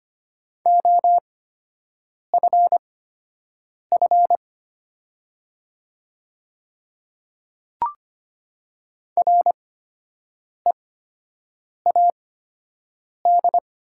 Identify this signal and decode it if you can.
Morse code, heard as a keyed tone, about 25 wpm, spaced as for 7 wpm, 700 Hz. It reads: OFF READ
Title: 8x Character Spacing - Top 200 Words - 25wpm